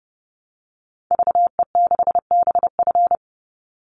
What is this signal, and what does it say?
Morse code, heard as a keyed tone, about 28 wpm, 700 Hz. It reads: VE6BF
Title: Call Signs Repeated 2x - 30wpm